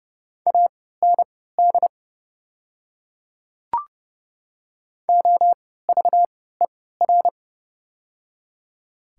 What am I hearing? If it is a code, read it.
Morse code, heard as a keyed tone, about 31 wpm, spaced as for 17 wpm, 700 Hz. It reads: AND OVER
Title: 3x Character Spacing - Top 300 Words - 30wpm